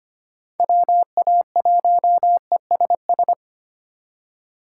Morse code, heard as a keyed tone, about 25 wpm, 700 Hz. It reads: WA1ESS
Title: USA Call Signs - 6 Characters - 2x Character Spacing - 25wpm